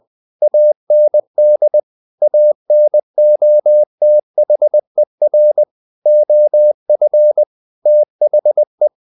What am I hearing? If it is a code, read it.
Morse code, heard as a keyed tone, about 20 wpm, 600 Hz. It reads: AND ANOTHER OF THE
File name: the old man and the sea / the old man and the sea part 1 20wpm